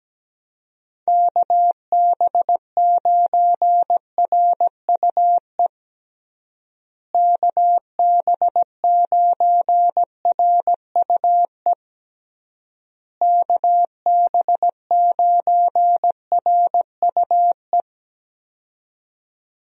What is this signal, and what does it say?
Morse code, heard as a keyed tone, about 17 wpm, 700 Hz. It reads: KB9RUE KB9RUE KB9RUE
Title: USA Call Signs - 6 Characters - Repeated 3x - 17wpm